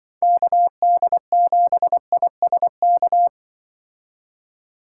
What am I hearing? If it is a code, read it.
Morse code, heard as a keyed tone, about 24 wpm, 700 Hz. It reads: KD7ISK
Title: Speed Racing - USA Call Signs 20wpm